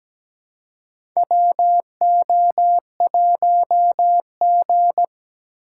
Morse code, heard as a keyed tone, about 17 wpm, 700 Hz. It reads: WO1G
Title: USA Call Signs - 4 Characters - 4x Character Spacing - 17wpm